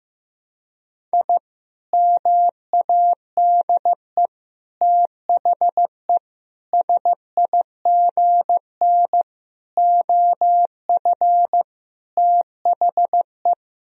Morse code, heard as a keyed tone, about 15 wpm, 700 Hz. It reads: I MADE THE SIGN OF THE
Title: Sentences from Top 1000 Words - Review Part 1 - 15wpm